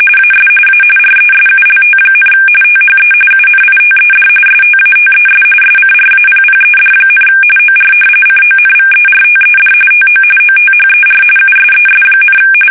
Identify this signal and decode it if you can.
RTTY, about 45.45 baud. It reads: AS A RESULT, THE UNITED STATES SCRAPPED, OR CONVERTED TO OTHER USES,
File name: fsk850 / battleshipnj